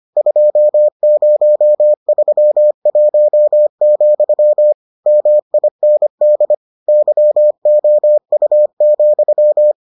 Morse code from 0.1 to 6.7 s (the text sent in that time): 2031, MIND